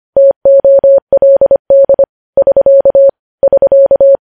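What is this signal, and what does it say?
Morse code, heard as a keyed tone, about 25 wpm, 550 Hz. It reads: TOLD <SK> <SK>
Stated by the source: Morse news headlines sent at 25 words per minute.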